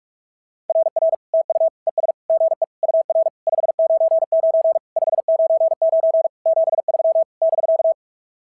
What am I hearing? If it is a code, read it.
Morse code, heard as a keyed tone, about 46 wpm, 650 Hz. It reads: RR TU ES GE UR 599 599 73 <BK>
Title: SOTA QSO Practice 45wpm